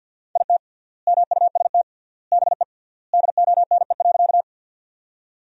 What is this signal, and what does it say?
Morse code, heard as a keyed tone, about 51 wpm, 700 Hz. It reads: IT MUST BE DONE.
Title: 3x Word-Spacing - Sentences from Top 500 Words - 50wpm